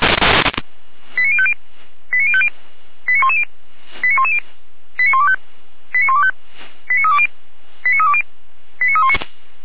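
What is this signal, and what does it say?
Five-tone selective call (ZVEI), heard as quick runs of five tones, tones of about 70 ms. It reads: 89050 89050 89110 89110 89125 89125 89320 89320 89319
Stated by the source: ZVEI five-tone selective call 89042 received on SDR.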